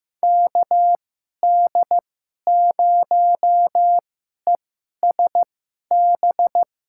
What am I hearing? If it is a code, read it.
Morse code, heard as a keyed tone, about 15 wpm, spaced as for 11 wpm, 700 Hz. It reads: KD0ESB
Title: USA Call Signs - 2x Character Spacing - 15wpm